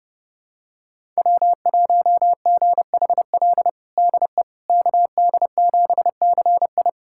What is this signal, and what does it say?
Morse code, heard as a keyed tone, about 30 wpm, 700 Hz. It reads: W1GHL DE KD7CI
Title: USA Callsign DE Callsign - 5 Characters - 30wpm